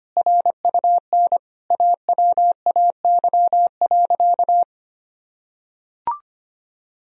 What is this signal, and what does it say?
Morse code, heard as a keyed tone, about 25 wpm, 700 Hz. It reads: RUN AWAY.
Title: Sentences from Top 1000 Words - Review Part 1 - 25wpm